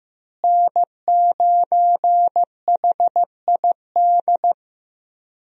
Morse code, heard as a keyed tone, about 15 wpm, 700 Hz. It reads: N9HID
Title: USA Call Signs - 5 Characters - Repeated 2x - 15wpm